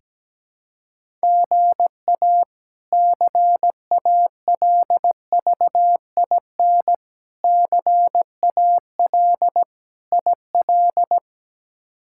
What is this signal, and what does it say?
Morse code, heard as a keyed tone, about 17 wpm, 700 Hz. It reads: GA CALVIN CAL IL
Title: K1USN SST QSO Practice - 17wpm